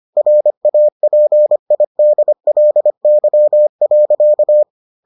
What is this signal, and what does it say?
Morse code, heard as a keyed tone, about 25 wpm, 600 Hz. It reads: RAPIDLY.